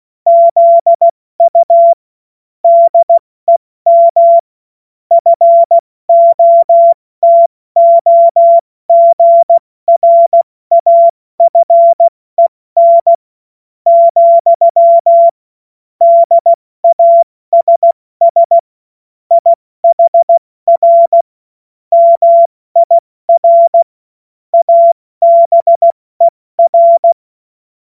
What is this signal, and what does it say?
Morse code, heard as a keyed tone, about 16 wpm, spaced as for 14 wpm, 680 Hz. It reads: ZU DEM FOTOGRAFEN, DASS IHR MIR ABER